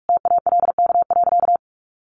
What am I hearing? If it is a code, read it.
Morse code, heard as a keyed tone, about 44 wpm, 700 Hz. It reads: TALK.
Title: Sentences from Top 1400 Words 45wpm